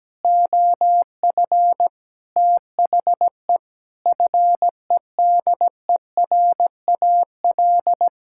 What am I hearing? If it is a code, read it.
Morse code, heard as a keyed tone, about 17 wpm, 700 Hz. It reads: OF THE FEDERAL
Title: Sentences from Top 1000 Words - Review Part 6 - 17wpm